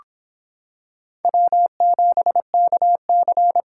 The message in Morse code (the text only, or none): W7KC